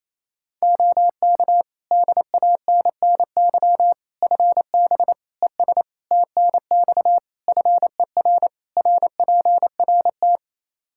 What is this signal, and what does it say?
Morse code, heard as a keyed tone, about 28 wpm, 700 Hz. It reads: OK DANNY FB ES TNX FER RPRT
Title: Practice QSO - Exchange 2 - Info - 28wpm